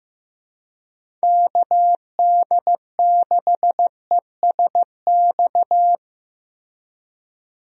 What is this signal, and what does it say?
Morse code, heard as a keyed tone, about 15 wpm, 700 Hz. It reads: KD6ESX
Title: USA Call Signs - 6 Characters - Repeated 2x - 15wpm